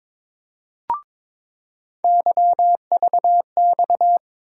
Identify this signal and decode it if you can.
Morse code, heard as a keyed tone, about 22 wpm, 700 Hz. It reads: YVX
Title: USA Call Sign Postfixes - 3 Letters 22wpm